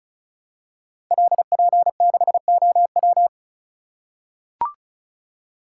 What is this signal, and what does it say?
Morse code, heard as a keyed tone, about 36 wpm, 700 Hz. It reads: LP6OW
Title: Code Groups - 5 Letter:Numbers - 35wpm